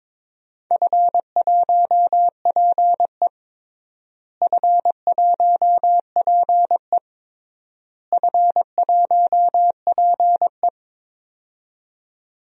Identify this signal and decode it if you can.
Morse code, heard as a keyed tone, about 22 wpm, 700 Hz. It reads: F1PE F1PE F1PE
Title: DXCC-100 Call Signs - 4 Characters - Repeated 3x - 22wpm